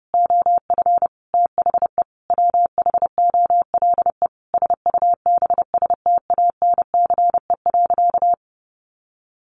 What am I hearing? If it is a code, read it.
Morse code, heard as a keyed tone, about 30 wpm, 700 Hz. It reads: OF THE WHOLE SUBSTANCE.
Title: Sentences from Top 1500 Words 30wpm